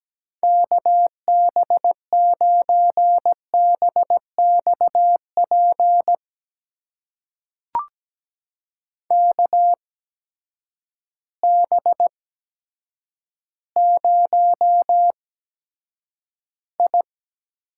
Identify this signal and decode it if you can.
Morse code, heard as a keyed tone, about 17 wpm, spaced as for 8 wpm, 700 Hz. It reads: KB9BXP KB0I
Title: USA Call Signs - 6 Characters - 8x Character Spacing - 17wpm